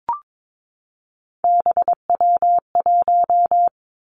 Morse code, heard as a keyed tone, about 22 wpm, 700 Hz. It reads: BW1
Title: Three Challenging Letter-Number Combinations 22wpm